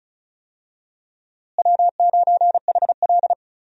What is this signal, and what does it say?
Morse code, heard as a keyed tone, about 34 wpm, 700 Hz. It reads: W9HL
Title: USA Call Signs - 4 Characters - Repeated 2x - 35wpm